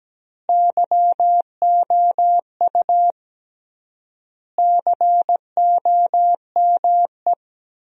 Morse code, heard as a keyed tone, about 17 wpm, 700 Hz. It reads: YOU COME
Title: 3x Word-Spacing - Sentences from Top 200 Words - 17wpm